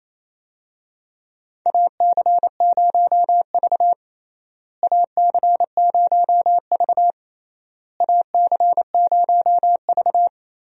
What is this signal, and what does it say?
Morse code, heard as a keyed tone, about 28 wpm, 700 Hz. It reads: AC0V AC0V AC0V
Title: USA Call Signs - 4 Characters - Repeated 3x - 28wpm